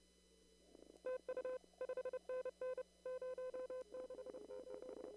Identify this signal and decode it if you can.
Morse code, heard as a keyed tone, about 29 wpm, 510 Hz. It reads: TU 5NN 068
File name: CW / kt5j wpx cw-op-k5pi